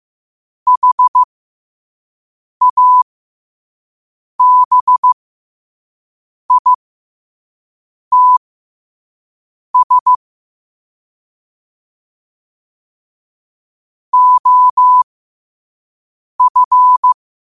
Morse code, heard as a keyed tone, about 15 wpm, spaced as for 5 wpm, 1000 Hz. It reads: HABITS OF